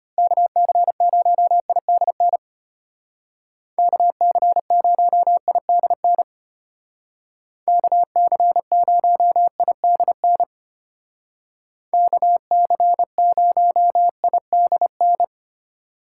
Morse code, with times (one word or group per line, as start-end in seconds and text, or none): 0.2-2.4 KC0IDN
3.8-6.2 KC0IDN
7.7-10.4 KC0IDN
11.9-15.2 KC0IDN